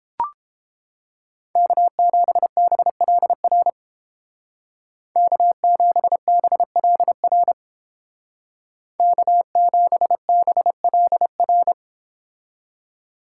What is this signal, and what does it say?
Morse code, heard as a keyed tone, about 31 wpm, 700 Hz. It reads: K7BLR K7BLR K7BLR